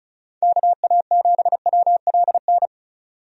Morse code, heard as a keyed tone, about 36 wpm, 700 Hz. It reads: KA7WLN
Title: USA Call Signs - 6 Characters - 35wpm